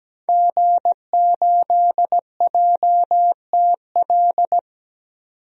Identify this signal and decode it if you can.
Morse code, heard as a keyed tone, about 17 wpm, 700 Hz. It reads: G8JTL
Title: DXCC-100 Call Signs - 5 Characters - 17wpm